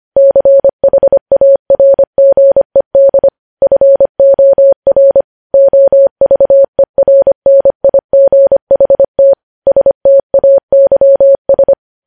Morse code, heard as a keyed tone, about 25 wpm, 550 Hz. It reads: CHARGED FOR OVERNIGHT STAYS
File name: podcast / News headlines-1x-25x25